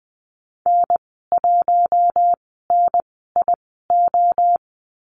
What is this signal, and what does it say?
Morse code, heard as a keyed tone, about 20 wpm, spaced as for 15 wpm, 700 Hz. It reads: N1NIO
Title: 048 Lesson 9g - Number 4 - Call Signs 20wpm